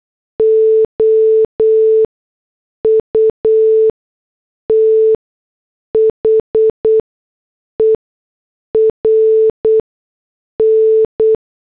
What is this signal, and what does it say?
Morse code, heard as a keyed tone, about 8 wpm, spaced as for 6 wpm, 440 Hz. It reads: OUTHERN
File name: morse / headlines